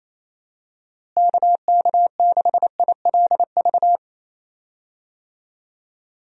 Morse code, heard as a keyed tone, about 28 wpm, 700 Hz. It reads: KK6ILV
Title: USA Call Signs - 6 Characters - Repeated 2x - 28wpm